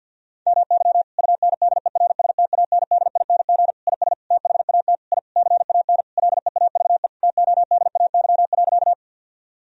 Morse code, heard as a keyed tone, about 51 wpm, 700 Hz. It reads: MY UNDERSTANDING IS THAT I CAN LEAVE TODAY.
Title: Sentences from Top 1000 Words - Review Part 3 - 50wpm